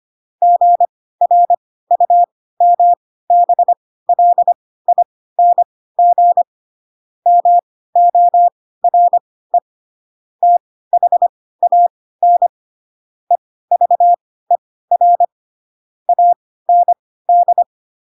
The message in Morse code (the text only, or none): GRUMBLING MORE THAN EVER AND